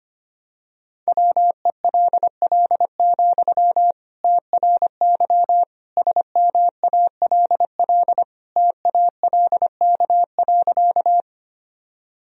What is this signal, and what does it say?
Morse code, heard as a keyed tone, about 25 wpm, 700 Hz. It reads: WELL, TRY SMALL TALK.
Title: Sentences from Top 1000 Words - Review Part 1 - 25wpm